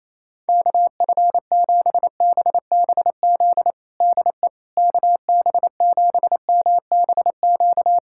Morse code, heard as a keyed tone, about 28 wpm, 700 Hz. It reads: KF7BBZ DE KB7MBQ
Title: USA Callsign DE Callsign - 6 Characters - 28wpm